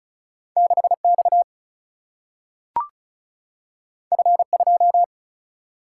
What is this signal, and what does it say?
Morse code, heard as a keyed tone, about 36 wpm, 700 Hz. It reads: 6X F2
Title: Single-Letter Single-Number 35wpm